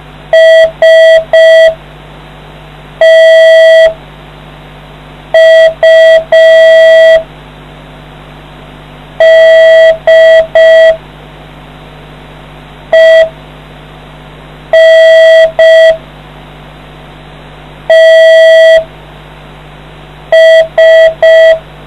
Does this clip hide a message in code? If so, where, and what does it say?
Yes: Morse code from 0.3 to 21.5 s, heard as a keyed tone, about 5 wpm, spaced as for 3 wpm, 650 Hz. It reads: STUDENTS